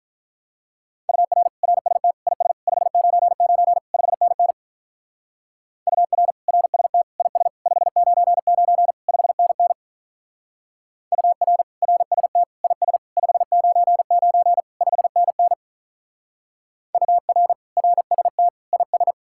Morse code, heard as a keyed tone, about 43 wpm, 700 Hz. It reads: UR RST IS 599 5NN UR RST IS 599 5NN UR RST IS 599 5NN UR RST IS
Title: Speed-Racing Common QSO Phrases Encore - 35wpm